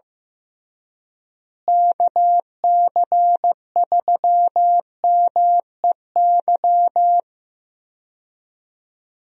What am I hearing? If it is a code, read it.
Morse code, heard as a keyed tone, about 15 wpm, 700 Hz. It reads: KC3MEY
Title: USA Call Signs - 6 Characters - Repeated 3x - 15wpm